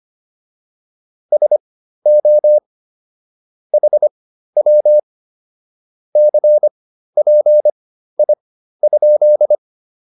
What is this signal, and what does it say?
Morse code, heard as a keyed tone, about 25 wpm, spaced as for 13 wpm, 600 Hz. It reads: SO HW CPI?